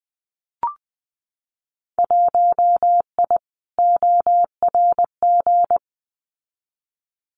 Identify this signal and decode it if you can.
Morse code, heard as a keyed tone, about 20 wpm, 700 Hz. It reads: 1I ORG
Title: Winter Field Day Exchanges 20wpm